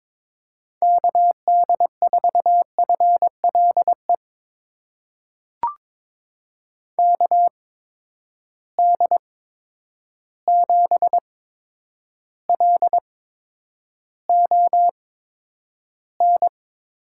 Morse code, heard as a keyed tone, about 22 wpm, spaced as for 9 wpm, 700 Hz. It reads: KD4FLE KD7LON
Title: USA Call Signs - 6 Characters - 8x Character Spacing - 22wpm